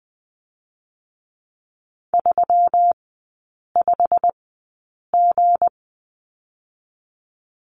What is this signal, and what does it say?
Morse code, heard as a keyed tone, about 20 wpm, spaced as for 8 wpm, 700 Hz. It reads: 35G
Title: 184 Lesson 30e - Letter b - Three Characters 20wpm